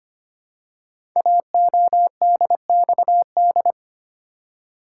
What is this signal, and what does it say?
Morse code, heard as a keyed tone, about 25 wpm, 700 Hz. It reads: AODXD
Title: Code Groups - 5 Letters - 25wpm